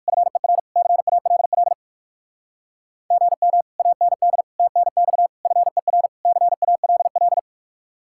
Morse code, heard as a keyed tone, about 56 wpm, 700 Hz. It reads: FER CALL GM AND TNX FER CALL